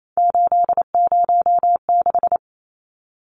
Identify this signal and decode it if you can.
Morse code, heard as a keyed tone, about 28 wpm, 700 Hz. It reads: 806
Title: Numbers - 3-Digit 28wpm